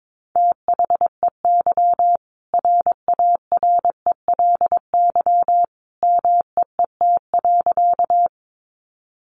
Morse code, heard as a keyed tone, about 22 wpm, 700 Hz. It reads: THEY RARELY MEET.
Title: Sentences from Top 1400 Words 22wpm